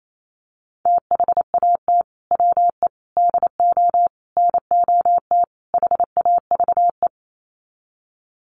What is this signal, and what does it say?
Morse code, heard as a keyed tone, about 28 wpm, 700 Hz. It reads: THAT WE DO NOT HAVE
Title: Sets of 5 Words Encore 28wpm